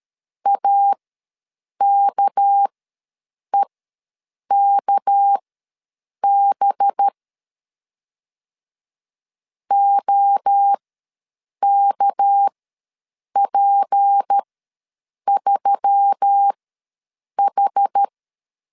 Morse code, heard as a keyed tone, about 13 wpm, spaced as for 7 wpm, 800 Hz. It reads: AKEKB OKP3H
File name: LEZIONI / 6 PKX0 / 6 PK ws 12wpm 3116